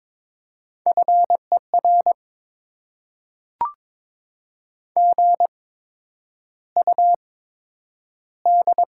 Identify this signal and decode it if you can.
Morse code, heard as a keyed tone, about 22 wpm, spaced as for 9 wpm, 700 Hz. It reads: FER GUD
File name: Top 20 QSO Elements - 8x Character Spacing - 22wpm